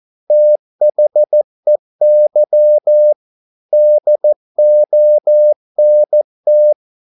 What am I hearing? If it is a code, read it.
Morse code, heard as a keyed tone, about 14 wpm, 600 Hz. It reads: THEY DONT